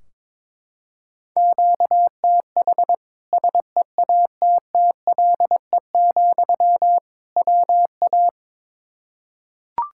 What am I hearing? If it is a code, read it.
Morse code, heard as a keyed tone, about 22 wpm, 700 Hz. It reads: QTH SEATTLE, WA
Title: Practice QSO - Exchange 1 - Beginning - 22wpm